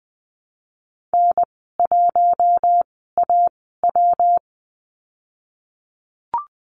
Morse code, heard as a keyed tone, about 20 wpm, spaced as for 15 wpm, 700 Hz. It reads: N1AW
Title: 147 Lesson 24g - Letter y - Call Signs 20wpm